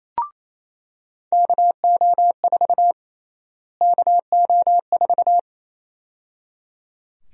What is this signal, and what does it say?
Morse code, heard as a keyed tone, about 28 wpm, 700 Hz. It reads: KO4 KO4
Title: USA Call Sign Prefixes - Repeated 2x - 28wpm